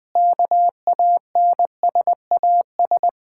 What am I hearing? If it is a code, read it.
Morse code, heard as a keyed tone, about 20 wpm, 700 Hz. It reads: KANSAS
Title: Practice QSO - Exchange 1 - Beginning - 20wpm